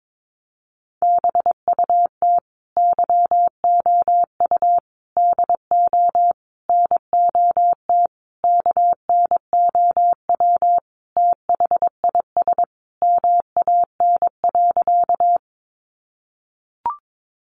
Morse code, heard as a keyed tone, about 22 wpm, 700 Hz. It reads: BUT YOU DO NOT KNOW THIS MAN.
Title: Sentences from Top 600 Words 22wpm